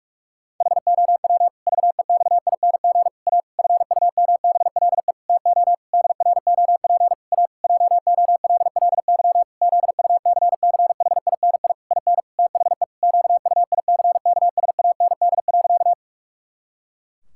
Morse code, heard as a keyed tone, about 46 wpm, 700 Hz. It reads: HOW VEXING A FUMBLE TO DROP A JOLLY ZUCCHINI IN THE QUICKSAND.
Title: Pangram Challenge v2 - 45wpm